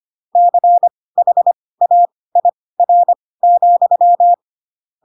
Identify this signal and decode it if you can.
Morse code, heard as a keyed tone, about 25 wpm, spaced as for 18 wpm, 700 Hz. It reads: CHAIR,